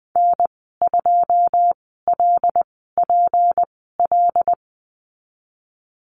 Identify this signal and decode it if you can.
Morse code, heard as a keyed tone, about 20 wpm, spaced as for 15 wpm, 700 Hz. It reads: N2LPL
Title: 186 Lesson 30g - Letter b - Call Signs 20wpm